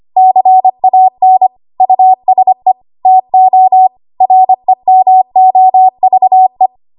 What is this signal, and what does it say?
Morse code, heard as a keyed tone, about 25 wpm, 750 Hz. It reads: CAN USE TO REMOVE
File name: Archive / 25 WPM / 25WPM